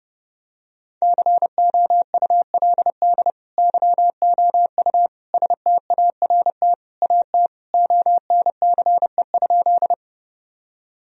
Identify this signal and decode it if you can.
Morse code, heard as a keyed tone, about 32 wpm, 700 Hz. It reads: COULD YOU START AT ONCE?
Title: Sentences from Top 1000 Words - Review Part 1 - 30wpm